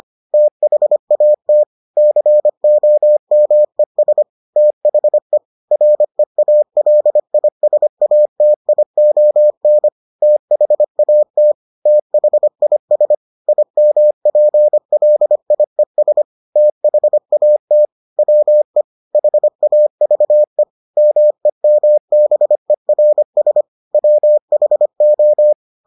Morse code, heard as a keyed tone, about 25 wpm, 600 Hz. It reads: THAT COMES THE REALISATION THAT THIS IMPLIES THAT WE HAVE MEMBERS WHO